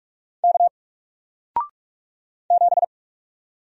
Morse code, heard as a keyed tone, about 44 wpm, 700 Hz. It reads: K 7
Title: Single Letter-Number 45wpm